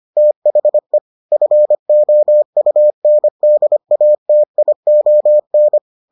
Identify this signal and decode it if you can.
Morse code, heard as a keyed tone, about 25 wpm, 600 Hz. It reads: THE FOUNDATION